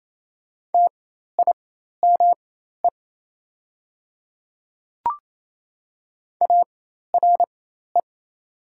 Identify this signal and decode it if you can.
Morse code, heard as a keyed tone, about 28 wpm, spaced as for 13 wpm, 700 Hz. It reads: TIME ARE
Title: 4x Character Spacing - Top 300 Words - 28wpm